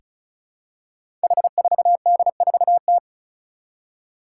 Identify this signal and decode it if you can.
Morse code, heard as a keyed tone, about 36 wpm, 700 Hz. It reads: H4D4T
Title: Five Challenging Letter-Number Combinations 35wpm